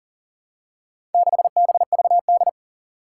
Morse code, heard as a keyed tone, about 40 wpm, 700 Hz. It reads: 6BVD
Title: Four Challenging Letter-Number Combinations 40wpm